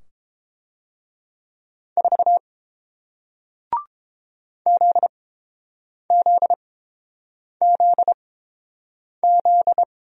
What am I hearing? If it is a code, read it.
Morse code, heard as a keyed tone, about 30 wpm, spaced as for 7 wpm, 700 Hz. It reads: 4 ZZZZ